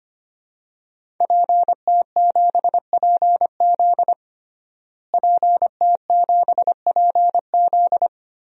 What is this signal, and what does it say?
Morse code, heard as a keyed tone, about 25 wpm, 700 Hz. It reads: PT7PZ PT7PZ
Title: DXCC-100 Call Signs - 5 Characters - Repeated 2x - 25wpm